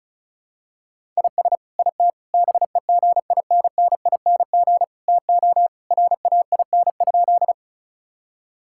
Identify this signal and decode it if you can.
Morse code, heard as a keyed tone, about 36 wpm, 700 Hz. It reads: IS IT BEGINNING TO RAIN?
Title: Sentences from Top 1000 Words - Review Part 5 - 35wpm